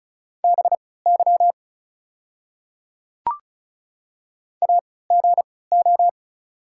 Morse code, heard as a keyed tone, about 34 wpm, spaced as for 20 wpm, 690 Hz. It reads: BY AGO
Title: 3x Character Spacing - Top 500 Words - 35wpm